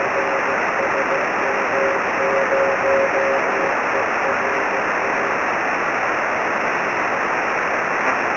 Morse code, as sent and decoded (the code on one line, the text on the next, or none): none